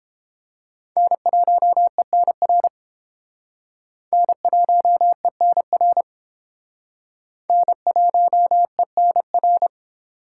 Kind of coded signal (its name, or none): Morse code